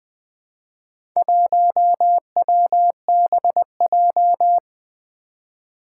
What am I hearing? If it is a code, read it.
Morse code, heard as a keyed tone, about 20 wpm, 700 Hz. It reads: 1WBJ
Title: Four Challenging Letter-Number Combinations 20wpm